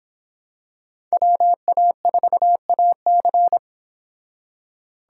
Morse code, heard as a keyed tone, about 26 wpm, 700 Hz. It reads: WA4AC